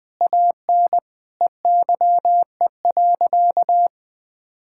Morse code, heard as a keyed tone, about 20 wpm, 700 Hz. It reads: AN EYE.
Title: Sentences with 6-Word Phrases - 20wpm